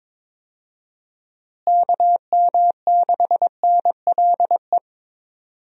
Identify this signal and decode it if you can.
Morse code, heard as a keyed tone, about 22 wpm, 700 Hz. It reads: KM6NLE
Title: USA Call Signs - 6 Characters - Repeated 3x - 22wpm